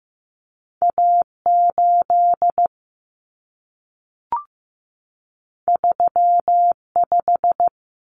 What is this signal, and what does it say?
Morse code, heard as a keyed tone, about 15 wpm, 700 Hz. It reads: A8 35
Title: Common Cut Numbers - 2-Digit 15wpm